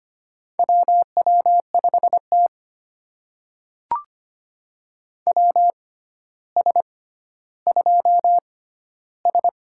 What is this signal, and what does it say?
Morse code, heard as a keyed tone, about 25 wpm, spaced as for 13 wpm, 700 Hz. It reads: WW5T WS2S
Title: USA Call Signs - 4 Characters - 6x Character Spacing - 25wpm